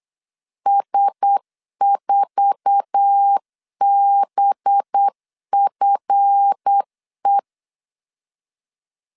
Morse code, heard as a keyed tone, about 8 wpm, 800 Hz. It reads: S4BFE
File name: CW / DET / 1 9 1500C S C 8wpm 5653